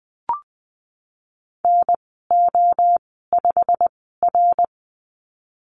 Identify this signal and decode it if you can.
Morse code, heard as a keyed tone, about 20 wpm, spaced as for 15 wpm, 700 Hz. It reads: NO5R
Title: 088 Lesson 15g - Number 5 - Call Signs 20wpm